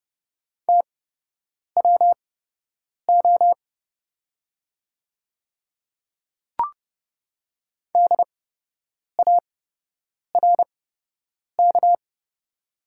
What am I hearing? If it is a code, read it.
Morse code, heard as a keyed tone, about 30 wpm, spaced as for 8 wpm, 700 Hz. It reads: TWO DARK
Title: 8x Character Spacing - Top 500 Words - 30wpm